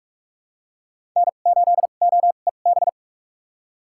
Morse code, heard as a keyed tone, about 47 wpm, spaced as for 33 wpm, 700 Hz. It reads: N8OEB
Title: USA Call Signs - 5 Characters - 2x Character Spacing - 45wpm